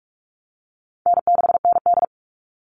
Morse code, heard as a keyed tone, about 44 wpm, 700 Hz. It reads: N6ND
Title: All Cut Numbers - 4-Digit 45wpm